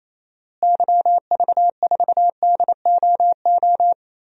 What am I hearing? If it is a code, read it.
Morse code, heard as a keyed tone, about 28 wpm, 700 Hz. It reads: YV4DOO
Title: DXCC-100 Call Signs - 6 Characters - Repeated 3x - 28wpm